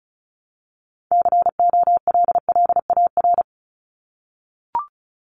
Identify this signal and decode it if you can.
Morse code, heard as a keyed tone, about 34 wpm, 700 Hz. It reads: COLLAR
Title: Top 1751-2000 Words 35wpm